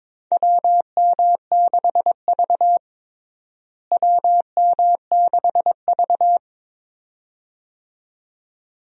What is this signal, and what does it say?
Morse code, heard as a keyed tone, about 22 wpm, 700 Hz. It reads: WM6V WM6V
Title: USA Call Signs - 4 Characters - Repeated 2x - 22wpm